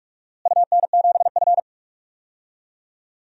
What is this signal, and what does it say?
Morse code, heard as a keyed tone, about 41 wpm, 700 Hz. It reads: UN7F
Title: DXCC-100 Call Signs - 4 Characters - Repeated 2x - 45wpm